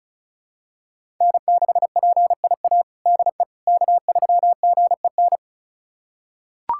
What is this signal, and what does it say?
Morse code, heard as a keyed tone, about 34 wpm, 700 Hz. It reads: N6PIA DE K3GEN
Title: USA Callsign DE Callsign - 5 Characters - 35wpm